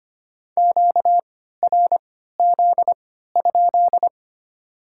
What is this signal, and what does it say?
Morse code, heard as a keyed tone, about 25 wpm, spaced as for 14 wpm, 690 Hz. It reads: QRZ?
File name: Top 90 QSO Elements - 3x Character Spacing - 25wpm